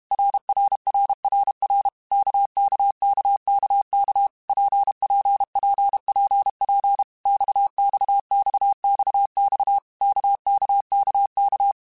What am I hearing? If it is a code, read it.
Morse code, heard as a keyed tone, about 32 wpm, 800 Hz. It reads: RRRRR KKKKK PPPPP XXXXX KKKK
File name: Basic Letter Groups By7Groups / RKPX / RKPX@30